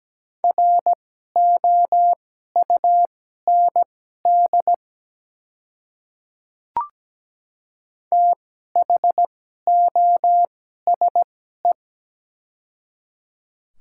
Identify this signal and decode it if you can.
Morse code, heard as a keyed tone, about 17 wpm, spaced as for 12 wpm, 700 Hz. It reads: ROUND THOSE
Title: 2x Character Spacing - Top 300 Words - 17wpm